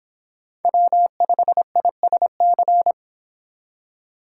Morse code, heard as a keyed tone, about 26 wpm, 700 Hz. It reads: W5ISC